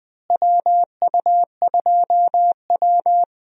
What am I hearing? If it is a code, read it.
Morse code, heard as a keyed tone, about 20 wpm, 700 Hz. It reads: WU2W
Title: USA Call Signs - 4 Characters - Repeated 3x - 20wpm